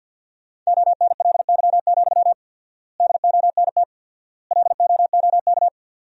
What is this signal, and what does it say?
Morse code, heard as a keyed tone, about 53 wpm, 700 Hz. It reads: ONLY, DONT LOOK
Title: 4x Word-Spacing - Sentences from Top 300 Words - 50wpm